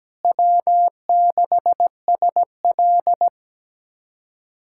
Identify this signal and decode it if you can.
Morse code, heard as a keyed tone, about 17 wpm, 700 Hz. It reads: W6SL
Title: USA Callsign DE Callsign - 4 Characters - 17wpm